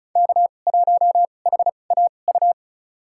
Morse code, heard as a keyed tone, about 36 wpm, spaced as for 25 wpm, 700 Hz. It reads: K1HAU
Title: USA Call Signs - 5 Characters - 2x Character Spacing - 35wpm